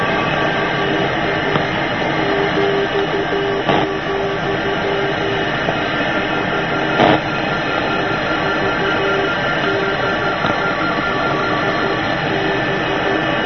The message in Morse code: none